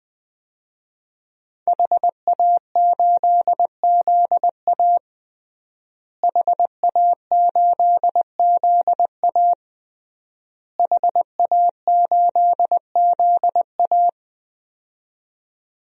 Morse code, heard as a keyed tone, about 21 wpm, 700 Hz. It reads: HA8ZA HA8ZA HA8ZA
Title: DXCC-100 Call Signs - 5 Characters - Repeated 3x - 20wpm